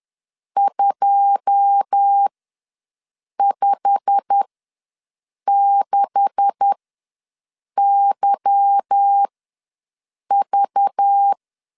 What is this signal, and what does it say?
Morse code, heard as a keyed tone, about 10 wpm, spaced as for 6 wpm, 800 Hz. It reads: 256YV
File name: LEZIONI / 9 286 ws 10wpm 4032